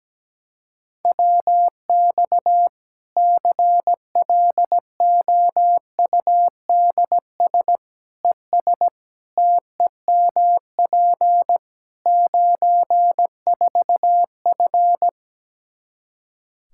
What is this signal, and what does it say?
Morse code, heard as a keyed tone, about 17 wpm, 700 Hz. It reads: WX CLOUDS ES TEMP 94F
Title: K1USN SST QSO Practice 17wpm